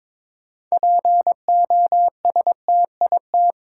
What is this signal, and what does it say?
Morse code, heard as a keyed tone, about 23 wpm, 700 Hz. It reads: POSTIT
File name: Difficult Words with Hints - 22wpm